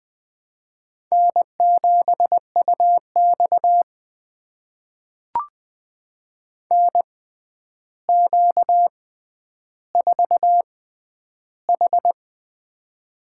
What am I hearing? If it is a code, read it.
Morse code, heard as a keyed tone, about 20 wpm, spaced as for 10 wpm, 700 Hz. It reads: N7UX NQ4H
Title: USA Call Signs - 4 Characters - 6x Character Spacing - 20wpm